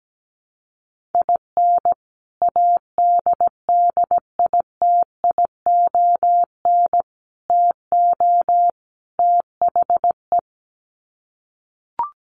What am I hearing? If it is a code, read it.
Morse code, heard as a keyed tone, about 17 wpm, 700 Hz. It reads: IN ADDITION TO THE
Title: Sets of 4 Words 17wpm